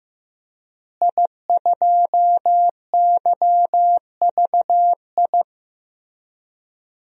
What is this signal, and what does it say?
Morse code, heard as a keyed tone, about 16 wpm, 700 Hz. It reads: I2YVI
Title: DXCC-100 Call Signs - 5 Characters - Repeated 3x - 15wpm